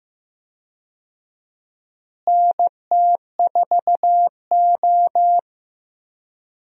Morse code, heard as a keyed tone, about 16 wpm, 700 Hz. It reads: NT4O
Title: USA Call Signs - 4 Characters - Repeated 2x - 15wpm